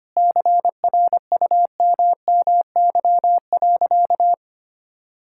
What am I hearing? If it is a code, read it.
Morse code, heard as a keyed tone, about 25 wpm, 700 Hz. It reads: CRUMMY.
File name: practice / 25wpm / All-Age Appropriate Jokes - 25wpm